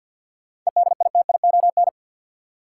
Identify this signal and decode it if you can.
Morse code, heard as a keyed tone, about 53 wpm, 700 Hz. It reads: EDITION
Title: Sets of 5 Words - New Word Drill - 50wpm